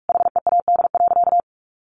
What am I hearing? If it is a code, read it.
Morse code, heard as a keyed tone, about 44 wpm, 700 Hz. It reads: HEAD.
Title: Easy Phrases 45wpm